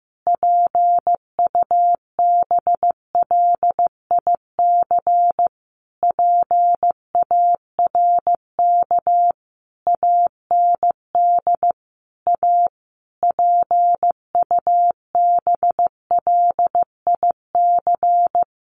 Morse code, heard as a keyed tone, about 14 wpm, 700 Hz. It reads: PUBLIC PARK AND A PUBLIC
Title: Sentences from Top 1300 Words 15wpm